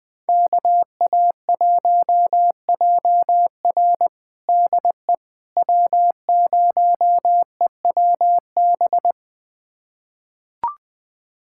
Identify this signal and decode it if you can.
Morse code, heard as a keyed tone, about 21 wpm, 700 Hz. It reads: KA1JR DE W0EWB
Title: USA Callsign DE Callsign - 5 Characters - 20wpm